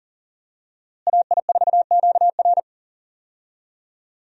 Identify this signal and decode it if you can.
Morse code, heard as a keyed tone, about 37 wpm, 700 Hz. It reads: AI4QR
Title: USA Call Signs - 5 Characters - Repeated 2x - 40wpm